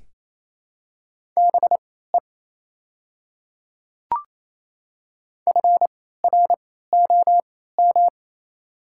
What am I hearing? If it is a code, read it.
Morse code, heard as a keyed tone, about 28 wpm, spaced as for 16 wpm, 700 Hz. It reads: BE FROM
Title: 3x Character Spacing - Top 300 Words - 28wpm